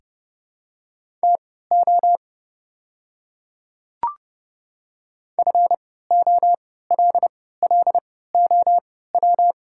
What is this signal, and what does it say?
Morse code, heard as a keyed tone, about 31 wpm, spaced as for 17 wpm, 700 Hz. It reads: TO FOLLOW
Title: 3x Character Spacing - Top 300 Words - 30wpm